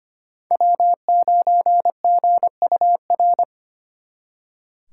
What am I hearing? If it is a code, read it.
Morse code, heard as a keyed tone, about 33 wpm, 700 Hz. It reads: W9GUR